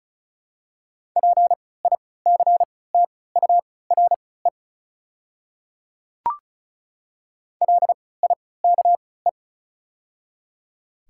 Morse code, heard as a keyed tone, about 34 wpm, spaced as for 20 wpm, 700 Hz. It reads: PICTURE LIKE
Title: 3x Character Spacing - Top 200 Words - 35wpm